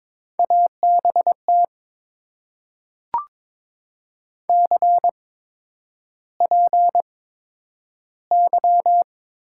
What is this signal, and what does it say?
Morse code, heard as a keyed tone, about 22 wpm, spaced as for 9 wpm, 700 Hz. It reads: ABT CPY
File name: Top 90 QSO Elements - 8x Character Spacing - 22wpm